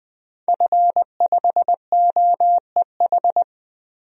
Morse code, heard as a keyed tone, about 20 wpm, 700 Hz. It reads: F5OEH
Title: DXCC-100 Call Signs - 5 Characters - 20wpm